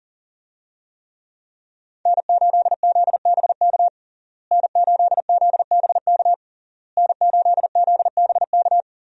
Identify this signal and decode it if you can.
Morse code, heard as a keyed tone, about 37 wpm, 700 Hz. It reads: N8ZBK N8ZBK N8ZBK
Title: USA Call Signs - 5 Characters - Repeated 3x - 40wpm